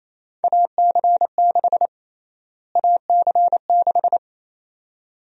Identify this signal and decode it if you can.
Morse code, heard as a keyed tone, about 27 wpm, 700 Hz. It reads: AC6 AC6
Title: USA Call Sign Prefixes - Repeated 2x - 28wpm